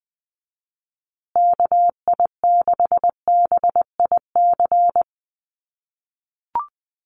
Morse code, heard as a keyed tone, about 20 wpm, 700 Hz. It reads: KI6BIC
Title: USA Call Signs Repeated 3x 20wpm